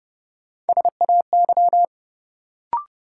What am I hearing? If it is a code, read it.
Morse code, heard as a keyed tone, about 30 wpm, 700 Hz. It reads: SAY
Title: Sentences from Top 200 Words - New Word Drill - 30wpm